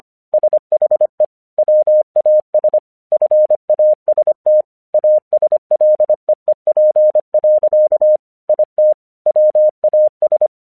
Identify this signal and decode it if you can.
Morse code, heard as a keyed tone, about 25 wpm, 610 Hz. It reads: SHE WAS FAST ASLEEP. IT WAS
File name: 1 English 25wpm / AQC28